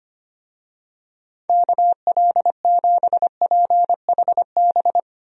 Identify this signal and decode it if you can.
Morse code, heard as a keyed tone, about 25 wpm, 700 Hz. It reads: KL7PHB
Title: DXCC-100 Call Signs - 6 Characters - 25wpm